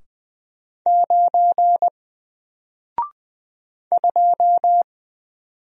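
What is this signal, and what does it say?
Morse code, heard as a keyed tone, about 20 wpm, 700 Hz. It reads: 9 2
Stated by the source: Single letter and number Morse drill at 20 wpm, from N4LCD.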